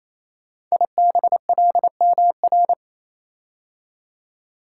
Morse code, heard as a keyed tone, about 29 wpm, 700 Hz. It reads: IBLMR
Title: Code Groups - 5 Letters - 28wpm